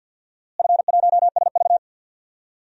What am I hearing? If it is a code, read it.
Morse code, heard as a keyed tone, about 50 wpm, 700 Hz. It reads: F1SV
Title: DXCC-100 Call Signs - 4 Characters - Repeated 2x - 50wpm